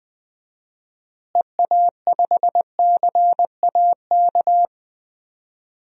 Morse code, heard as a keyed tone, about 20 wpm, 700 Hz. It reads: EA5CAK
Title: DXCC-100 Call Signs - 6 Characters - Repeated 3x - 20wpm